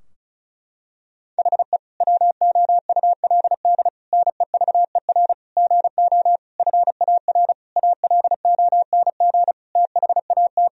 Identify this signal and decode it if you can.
Morse code, heard as a keyed tone, about 36 wpm, 700 Hz. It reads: HE WOULD NEVER GO FAR ALONG THAT
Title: Sentences from Top 1000 Words - Review Part 2 - 35wpm